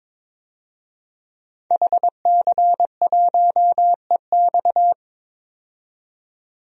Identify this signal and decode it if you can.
Morse code, heard as a keyed tone, about 22 wpm, 700 Hz. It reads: HC1EX
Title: DXCC-100 Call Signs - 5 Characters - 22wpm